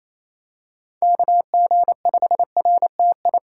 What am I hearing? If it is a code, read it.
Morse code, heard as a keyed tone, about 28 wpm, 700 Hz. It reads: KG5RTI
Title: USA Callsign DE Callsign 28wpm